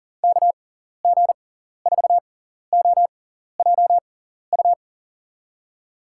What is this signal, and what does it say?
Morse code, heard as a keyed tone, about 40 wpm, spaced as for 14 wpm, 700 Hz. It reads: KG4OJU
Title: USA Call Signs - 6 Characters - 6x Character Spacing - 40wpm